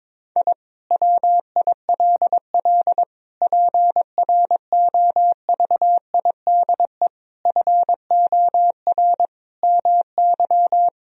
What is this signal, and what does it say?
Morse code, heard as a keyed tone, about 22 wpm, 700 Hz. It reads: I WILL PROVIDE FOR MY
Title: Sentences from Top 1000 Words - Review Part 3 - 22wpm